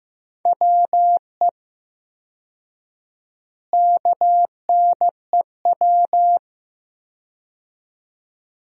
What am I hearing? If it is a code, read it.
Morse code, heard as a keyed tone, about 15 wpm, 700 Hz. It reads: WE KNEW
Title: 4x Word-Spacing - Sentences from Top 400 Words - 15wpm